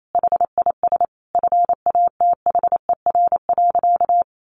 Morse code, heard as a keyed tone, about 28 wpm, 700 Hz. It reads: HIS FATHER.
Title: Sentences from Top 1500 Words 28wpm